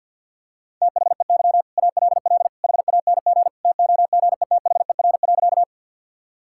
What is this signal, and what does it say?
Morse code, heard as a keyed tone, about 50 wpm, 700 Hz. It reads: THEY ALL HANG TOGETHER.
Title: Sentences from Top 1600 Words 50wpm